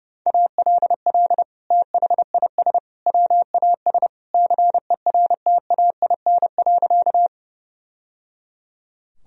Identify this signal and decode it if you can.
Morse code, heard as a keyed tone, about 32 wpm, 700 Hz. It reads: ALL THIS WAS CERTAIN.
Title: Sentences from Top 1000 Words - Review Part 2 - 30wpm